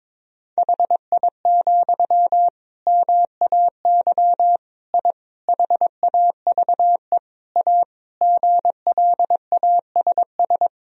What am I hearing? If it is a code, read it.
Morse code, heard as a keyed tone, about 22 wpm, 700 Hz. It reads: HI, MAY I HAVE A GLASS
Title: Sentences from Top 1000 Words - Review Part 3 - 22wpm